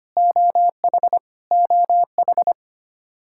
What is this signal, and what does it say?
Morse code, heard as a keyed tone, about 25 wpm, 700 Hz. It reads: OH OH
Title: Practice QSO - Exchange 1 - Beginning - 25wpm